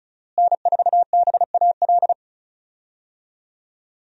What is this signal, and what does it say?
Morse code, heard as a keyed tone, about 36 wpm, 700 Hz. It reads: N4BAL
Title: USA Call Signs - 5 Characters - Repeated 3x - 35wpm